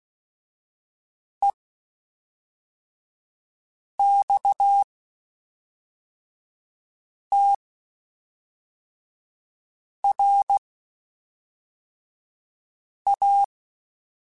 Morse code, heard as a keyed tone, about 16 wpm, spaced as for 3 wpm, 800 Hz. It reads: EXTRA
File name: (spaced) / QSOs Spaced-mp3s / kc7haa @5 sp